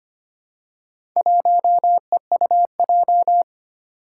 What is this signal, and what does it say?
Morse code, heard as a keyed tone, about 25 wpm, 700 Hz. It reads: 1EUJ
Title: Four Challenging Letter-Number Combinations 25wpm